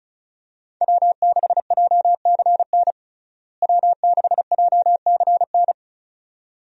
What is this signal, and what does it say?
Morse code, heard as a keyed tone, about 36 wpm, 700 Hz. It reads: W6JCN W6JCN
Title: USA Call Signs - 5 Characters - Repeated 2x - 35wpm